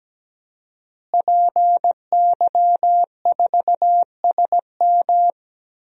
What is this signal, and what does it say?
Morse code, heard as a keyed tone, about 17 wpm, 700 Hz. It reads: PY4SM
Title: DXCC-100 Call Signs - 5 Characters - Repeated 3x - 17wpm